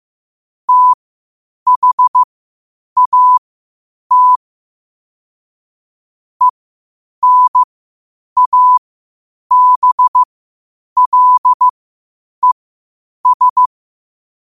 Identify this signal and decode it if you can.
Morse code, heard as a keyed tone, about 15 wpm, spaced as for 8 wpm, 1000 Hz. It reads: THAT ENABLES